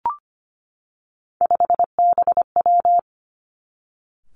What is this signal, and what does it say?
Morse code, heard as a keyed tone, about 25 wpm, 700 Hz. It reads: 5BW
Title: Three Challenging Letter-Number Combinations 25wpm